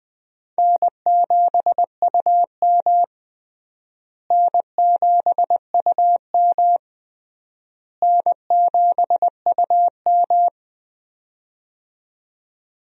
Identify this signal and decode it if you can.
Morse code, heard as a keyed tone, about 20 wpm, 700 Hz. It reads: N7UM N7UM N7UM
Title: USA Call Signs - 4 Characters - Repeated 3x - 20wpm